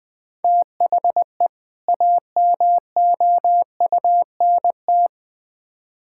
Morse code, heard as a keyed tone, about 20 wpm, 700 Hz. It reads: THE AMOUNT
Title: Build A Phrase - 3-Word Phrases - 20wpm